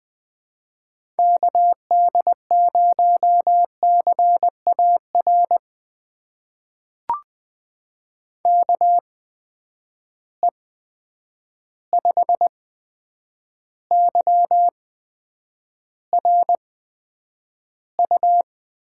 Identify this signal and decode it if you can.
Morse code, heard as a keyed tone, about 20 wpm, spaced as for 9 wpm, 700 Hz. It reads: KD0CAR KE5YRU
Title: USA Call Signs - 6 Characters - 8x Character Spacing - 20wpm